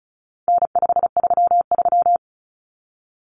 Morse code, heard as a keyed tone, about 34 wpm, 700 Hz. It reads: N533
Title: Common Cut Numbers - 4-Digit 35wpm